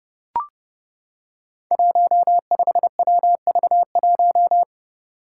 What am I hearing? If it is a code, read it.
Morse code, heard as a keyed tone, about 30 wpm, 700 Hz. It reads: 15WV1
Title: Five Challenging Letter-Number Combinations 30wpm